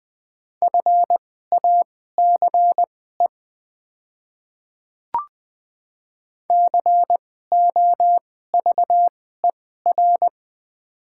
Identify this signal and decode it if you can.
Morse code, heard as a keyed tone, about 20 wpm, spaced as for 15 wpm, 700 Hz. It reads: FACE COVER
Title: 2x Character Spacing - Top 300 Words - 20wpm